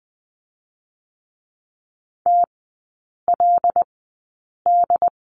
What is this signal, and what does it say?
Morse code, heard as a keyed tone, about 20 wpm, spaced as for 8 wpm, 700 Hz. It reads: TLD
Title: 074 Lesson 13e - Letter L - Three Characters 20wpm